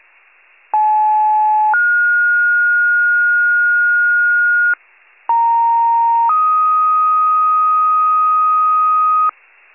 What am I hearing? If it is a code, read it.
Two-tone page, heard as a one-second tone, then a three-second tone, about 860 then 1430 Hz, about 930 then 1250 Hz, with noise. It is heard from 0.7 to 9.3 s.
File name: pager / 94 RESCUE OFC / wcalert